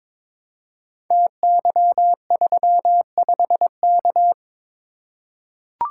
Morse code, heard as a keyed tone, about 22 wpm, 700 Hz. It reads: TY35K
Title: Code Groups - 5 Letter:Numbers - 22wpm